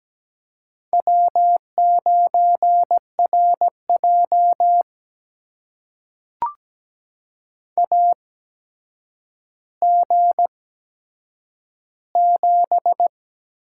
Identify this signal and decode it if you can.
Morse code, heard as a keyed tone, about 17 wpm, spaced as for 8 wpm, 700 Hz. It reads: W9RJ AG7
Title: USA Call Signs - 4 Characters - 8x Character Spacing - 17wpm